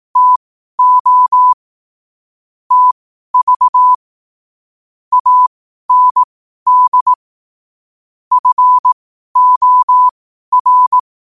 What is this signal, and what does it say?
Morse code, heard as a keyed tone, about 18 wpm, spaced as for 12 wpm, 1000 Hz. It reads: TO TV AND FOR
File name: qotdfiles / 12wpm